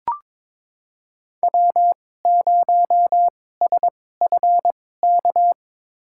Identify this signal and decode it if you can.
Morse code, heard as a keyed tone, about 22 wpm, spaced as for 16 wpm, 700 Hz. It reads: W0SFK
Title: USA Call Signs - 5 Characters - 2x Character Spacing - 22wpm